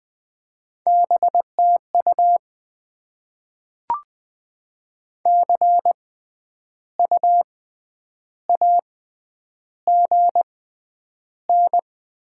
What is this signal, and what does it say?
Morse code, heard as a keyed tone, about 20 wpm, spaced as for 9 wpm, 690 Hz. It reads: BTU CU AGN
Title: Top 20 QSO Elements - 6x Character Spacing - 20wpm